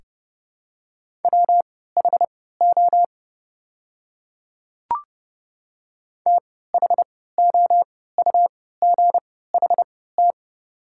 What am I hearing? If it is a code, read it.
Morse code, heard as a keyed tone, about 30 wpm, spaced as for 17 wpm, 700 Hz. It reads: WHO THOUGHT
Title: 3x Character Spacing - Top 500 Words - 30wpm